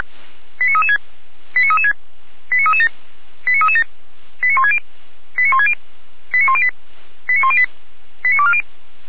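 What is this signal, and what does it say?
Five-tone selective call (ZVEI), heard as quick runs of five tones, tones of about 70 ms. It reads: 89336 89336 89337 89337 89160 89160 89108 89108 89240
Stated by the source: ZVEI five-tone call received by SDR.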